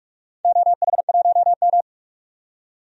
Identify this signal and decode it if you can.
Morse code, heard as a keyed tone, about 44 wpm, 700 Hz. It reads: OH1M
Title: DXCC-100 Call Signs - 4 Characters - 45wpm